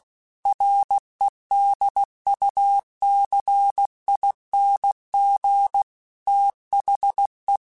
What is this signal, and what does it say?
Morse code, heard as a keyed tone, about 16 wpm, 800 Hz. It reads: REDUCING THE
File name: QSOs / QSOs-mp3s / WD0FBT 1@15